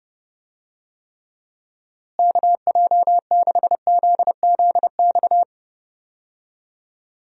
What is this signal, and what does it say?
Morse code, heard as a keyed tone, about 30 wpm, 700 Hz. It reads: KJ6ZZX
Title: USA Call Signs - 6 Characters - Repeated 2x - 30wpm